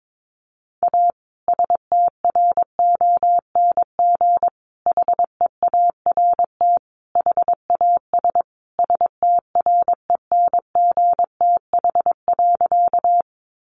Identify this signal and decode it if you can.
Morse code, heard as a keyed tone, about 22 wpm, 700 Hz. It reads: A STRONG HEART HAS STRENGTH.
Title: Sentences from Top 1300 Words 22wpm